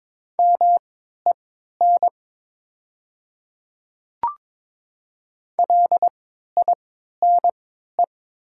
Morse code, heard as a keyed tone, about 22 wpm, spaced as for 12 wpm, 700 Hz. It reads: MEN LINE
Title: 3x Character Spacing - Top 200 Words - 22wpm